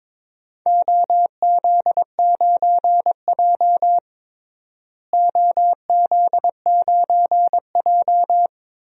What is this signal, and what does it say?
Morse code, heard as a keyed tone, about 21 wpm, 700 Hz. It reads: OZ9J OZ9J
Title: DXCC-100 Call Signs - 4 Characters - Repeated 2x - 22wpm